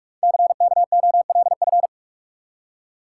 Morse code, heard as a keyed tone, about 44 wpm, 700 Hz. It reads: CKOLF
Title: Code Groups - 5 Letters - 45wpm